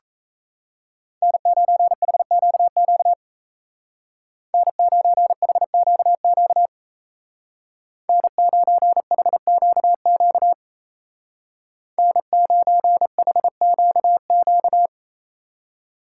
Morse code, heard as a keyed tone, about 34 wpm, 700 Hz. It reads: N9HQQ N9HQQ N9HQQ N9HQQ
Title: Speed Racing - USA 1x3 Call Signs 28wpm